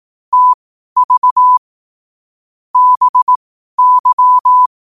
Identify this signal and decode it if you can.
Morse code, heard as a keyed tone, about 18 wpm, spaced as for 12 wpm, 1000 Hz. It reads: TV BY
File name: qotdfiles / 12wpm